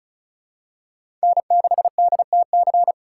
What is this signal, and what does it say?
Morse code, heard as a keyed tone, about 36 wpm, 700 Hz. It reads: N6DTC
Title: USA Call Signs - 5 Characters - 3x Character Spacing - 35wpm